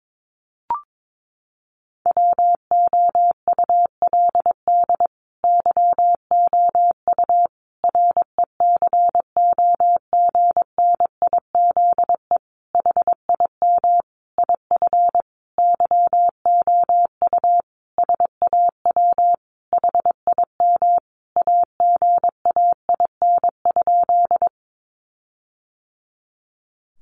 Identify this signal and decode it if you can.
Morse code, heard as a keyed tone, about 22 wpm, 700 Hz. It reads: WOULD YOU RECOGNIZE HIM IF YOU SAW HIM AGAIN?
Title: Sentences from Top 1400 Words 22wpm